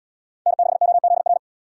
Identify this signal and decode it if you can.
Morse code, heard as a keyed tone, about 71 wpm, 700 Hz. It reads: N6OZR